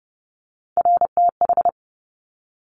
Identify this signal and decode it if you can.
Morse code, heard as a keyed tone, about 30 wpm, 700 Hz. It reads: RTH
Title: Top Three Letter Combinations in Words 30wpm